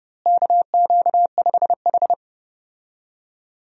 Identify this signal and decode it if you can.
Morse code, heard as a keyed tone, about 32 wpm, 700 Hz. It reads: KQ5H
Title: USA Call Signs - 4 Characters - Repeated 3x - 30wpm